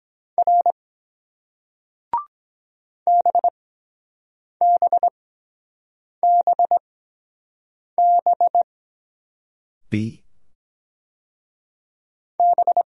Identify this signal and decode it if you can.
Morse code, heard as a keyed tone, about 23 wpm, spaced as for 7 wpm, 700 Hz. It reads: R BBBB B